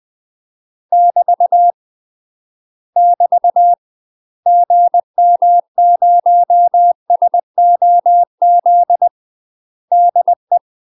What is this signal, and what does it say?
Morse code, heard as a keyed tone, about 20 wpm, 700 Hz. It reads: <BT> <BT> GM0SOZ DE
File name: UK MASTER QSOs TRAINING / 20WPM / UK MASTER QSOs 0048